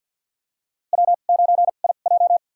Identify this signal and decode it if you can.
Morse code, heard as a keyed tone, about 53 wpm, spaced as for 37 wpm, 700 Hz. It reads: W9IJ
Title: USA Call Signs - 2x Character Spacing - 50wpm